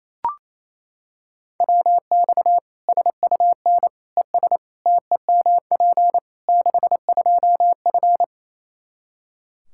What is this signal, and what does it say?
Morse code, heard as a keyed tone, about 28 wpm, 700 Hz. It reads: WX SUN ES TEMP 62F
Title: Practice QSO - Exchange 2 - Info - 28wpm